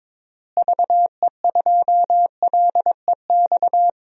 Morse code, heard as a keyed tone, about 22 wpm, 700 Hz. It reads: VE2LEX
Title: DXCC-100 Call Signs - 6 Characters - Repeated 2x - 22wpm